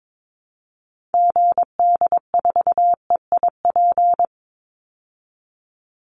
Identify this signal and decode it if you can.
Morse code, heard as a keyed tone, about 22 wpm, 690 Hz. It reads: GD4EIP
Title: Call Signs Repeated 3x - 22wpm